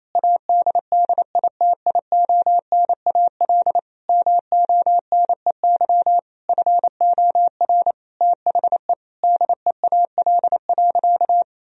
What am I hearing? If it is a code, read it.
Morse code, heard as a keyed tone, about 28 wpm, 700 Hz. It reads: ADDITIONAL MONEY FOR THE DEAL.
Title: Sentences from Top 1000 Words - Review Part 4 - 28wpm